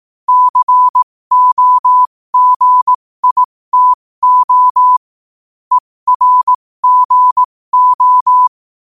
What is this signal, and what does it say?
Morse code, heard as a keyed tone, about 18 wpm, spaced as for 15 wpm, 1000 Hz. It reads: COGITO ERGO
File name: qotdfiles / 15wpm